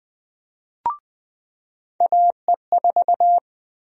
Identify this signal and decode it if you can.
Morse code, heard as a keyed tone, about 20 wpm, 700 Hz. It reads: AE4
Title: USA Call Sign Prefixes - 20wpm